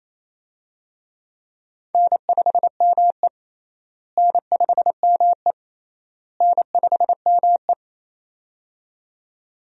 Morse code, heard as a keyed tone, about 28 wpm, 700 Hz. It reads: N5ME N5ME N5ME
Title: USA Call Signs - 4 Characters - Repeated 3x - 28wpm